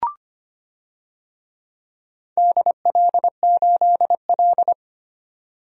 Morse code, heard as a keyed tone, about 25 wpm, 700 Hz. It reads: DL8L
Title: DXCC-100 Call Signs - 4 Characters - 25wpm